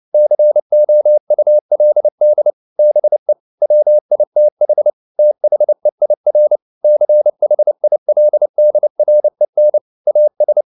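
Morse code, heard as a keyed tone, about 29 wpm, 600 Hz. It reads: COULD BE WITH THEIR CHILDREN AS
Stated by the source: Morse code at 29 wpm, sending Ken Follett's novel Never in English, from CW med Gnister.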